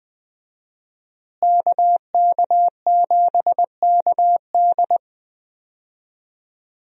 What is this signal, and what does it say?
Morse code, heard as a keyed tone, about 20 wpm, 700 Hz. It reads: KK7KD
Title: USA Call Signs - 5 Characters - Repeated 3x - 20wpm